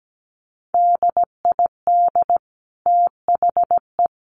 Morse code, heard as a keyed tone, about 17 wpm, 700 Hz. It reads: DID THE
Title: Sentences from Top 800 Words 17wpm